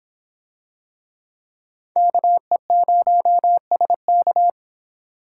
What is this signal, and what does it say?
Morse code, heard as a keyed tone, about 24 wpm, 700 Hz. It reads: KE0SK